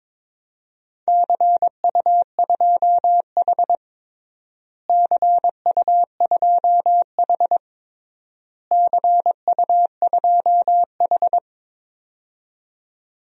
Morse code, heard as a keyed tone, about 22 wpm, 700 Hz. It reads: CU2H CU2H CU2H
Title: DXCC-100 Call Signs - 4 Characters - Repeated 3x - 22wpm